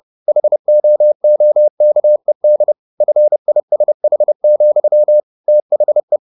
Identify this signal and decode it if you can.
Morse code, heard as a keyed tone, about 30 wpm, 600 Hz. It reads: HOOKED FISH, THE
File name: the old man and the sea / the old man and the sea 1 30wpm